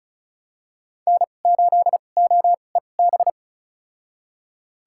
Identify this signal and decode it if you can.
Morse code, heard as a keyed tone, about 36 wpm, spaced as for 25 wpm, 700 Hz. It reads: N8OEB
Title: USA Call Signs - 5 Characters - 2x Character Spacing - 35wpm